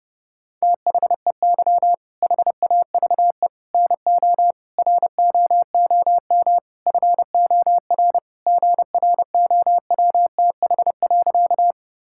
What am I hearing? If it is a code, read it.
Morse code, heard as a keyed tone, about 30 wpm, 700 Hz. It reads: THEY HAVE NO ROOM FOR GROWTH.
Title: Sentences from Top 1000 Words - Review Part 5 - 30wpm